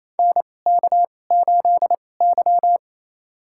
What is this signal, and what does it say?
Morse code, heard as a keyed tone, about 28 wpm, spaced as for 20 wpm, 700 Hz. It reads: NK8Y
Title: USA Call Signs - 4 Characters - 2x Character Spacing - 28wpm